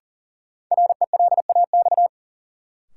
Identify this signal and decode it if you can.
Morse code, heard as a keyed tone, about 40 wpm, 700 Hz. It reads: RELAX
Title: Sentences from Top 1400 Words - New Word Drill Alt - 40wpm